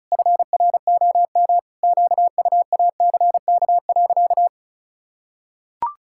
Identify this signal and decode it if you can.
Morse code, heard as a keyed tone, about 36 wpm, 700 Hz. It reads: FROM QUACK.
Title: Pangram Challenge v2 - 35wpm